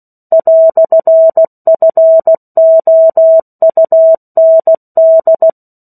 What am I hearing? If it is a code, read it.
Morse code, heard as a keyed tone, about 16 wpm, 650 Hz. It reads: "FOUND
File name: 03-Chapter-03 16 WPM